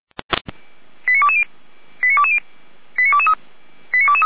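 Five-tone selective call (ZVEI), heard as quick runs of five tones, tones of about 80 ms. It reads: 89220 89220 89223 89223
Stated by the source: ZVEI five-tone selective call received on SDR.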